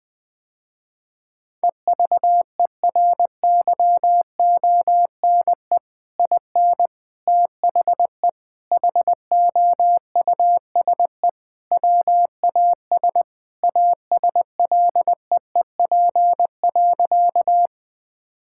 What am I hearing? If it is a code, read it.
Morse code, heard as a keyed tone, about 21 wpm, 700 Hz. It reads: EVERYONE IN THE HOUSE WAS ASLEEP.
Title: Sentences with 3-Word Phrases - Spoken Beginning - 20wpm